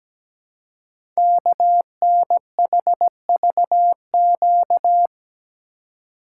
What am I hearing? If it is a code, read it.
Morse code, heard as a keyed tone, about 17 wpm, 700 Hz. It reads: KNHVQ
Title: Code Groups - 5 Letters - 17wpm